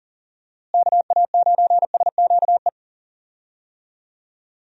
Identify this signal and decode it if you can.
Morse code, heard as a keyed tone, about 40 wpm, 700 Hz. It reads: KA9SQE
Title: USA Call Signs - 6 Characters - Repeated 2x - 40wpm